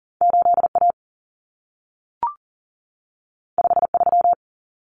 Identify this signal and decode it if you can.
Morse code, heard as a keyed tone, about 40 wpm, 700 Hz. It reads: 8A 53
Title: Common Cut Numbers - 2-Digit 40wpm